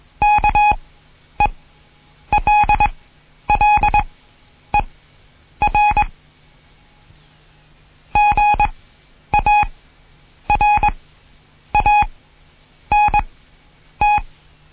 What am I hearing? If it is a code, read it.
Morse code, heard as a keyed tone, about 21 wpm, spaced as for 9 wpm, 830 Hz. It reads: KELLER GARANT
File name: Morsen / Kurs A 20